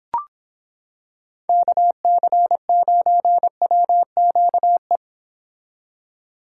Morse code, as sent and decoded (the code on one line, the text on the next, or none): -.- -.-. ----. .-- --.- .
KC9WQE